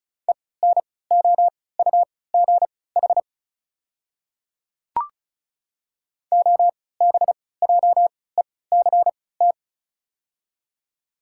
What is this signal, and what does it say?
Morse code, heard as a keyed tone, about 36 wpm, spaced as for 20 wpm, 690 Hz. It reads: ENOUGH OBJECT
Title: 3x Character Spacing - Top 500 Words - 35wpm